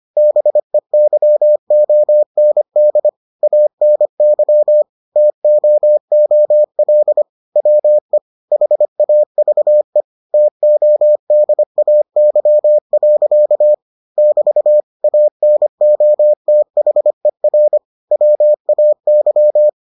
Morse code, heard as a keyed tone, about 25 wpm, 600 Hz. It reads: BEYOND ANY TOOL WE HAVE TODAY. <BT> ANOTHER WAY